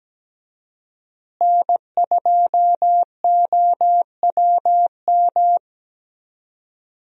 Morse code, heard as a keyed tone, about 17 wpm, 700 Hz. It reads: N2OWM
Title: USA Call Signs - 5 Characters - Repeated 3x - 17wpm